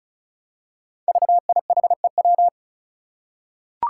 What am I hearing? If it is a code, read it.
Morse code, heard as a keyed tone, about 36 wpm, 700 Hz. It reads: VIHEW
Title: Code Groups - 5 Letter:Numbers - 35wpm